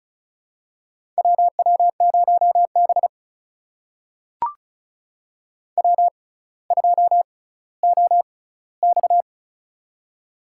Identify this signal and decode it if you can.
Morse code, heard as a keyed tone, about 36 wpm, spaced as for 18 wpm, 700 Hz. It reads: WW0B W2OX
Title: USA Call Signs - 4 Characters - 6x Character Spacing - 35wpm